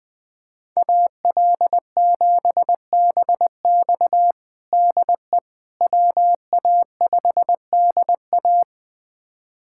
Morse code, heard as a keyed tone, about 20 wpm, 700 Hz. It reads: AL7BX DE WA5DA
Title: USA Callsign DE Callsign - 5 Characters - 20wpm